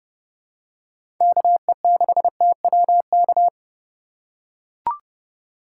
Morse code, heard as a keyed tone, about 30 wpm, 700 Hz. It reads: KE6TWK
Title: USA Call Signs - 6 Characters - 2x Character Spacing - 30wpm